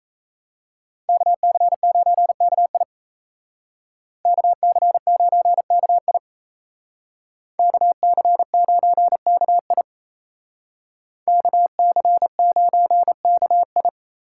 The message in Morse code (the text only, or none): KC9KI KC9KI KC9KI KC9KI